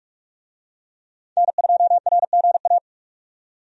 Morse code, heard as a keyed tone, about 47 wpm, 700 Hz. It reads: N2RGA
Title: USA Call Signs - 5 Characters - 45wpm